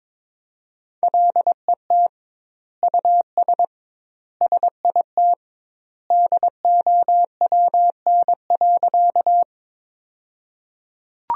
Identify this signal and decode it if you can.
Morse code, heard as a keyed tone, about 22 wpm, 700 Hz. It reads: LET US SIT DOWN.
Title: 2x Word-Spacing - Sentences from Top 500 Words - 22wpm